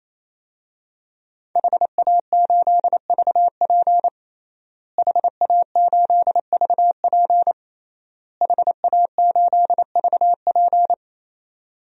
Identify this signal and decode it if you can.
Morse code, heard as a keyed tone, about 28 wpm, 700 Hz. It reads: HA8VP HA8VP HA8VP
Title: DXCC-100 Call Signs - 5 Characters - Repeated 3x - 28wpm